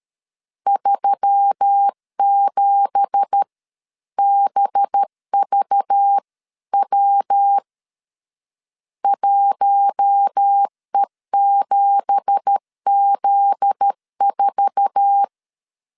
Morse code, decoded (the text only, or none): 37BVW 1E7Z4